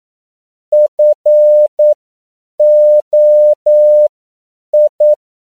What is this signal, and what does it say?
Morse code, heard as a keyed tone, about 9 wpm, spaced as for 7 wpm, 600 Hz. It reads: FOI
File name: CW / 10PPM